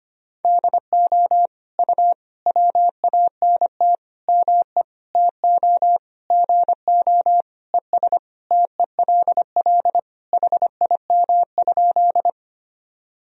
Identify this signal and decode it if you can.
Morse code, heard as a keyed tone, about 25 wpm, 700 Hz. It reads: DO U WANT ME TO GO ES TELL HIM?
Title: Sentences from Top 200 Words and Abbreviations 25wpm